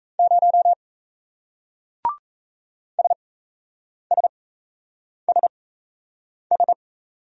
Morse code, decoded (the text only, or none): 0 SSSS